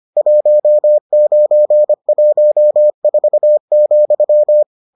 Morse code, heard as a keyed tone, about 25 wpm, 600 Hz. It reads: 1914,